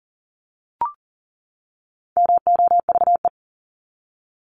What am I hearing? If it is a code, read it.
Morse code, heard as a keyed tone, about 40 wpm, 700 Hz. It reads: MOVE
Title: ING Suffix - Top 100 Words with Base Words 40wpm